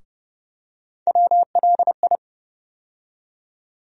Morse code, heard as a keyed tone, about 30 wpm, 700 Hz. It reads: WLI
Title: USA Call Sign Postfixes - 3 Letters 30wpm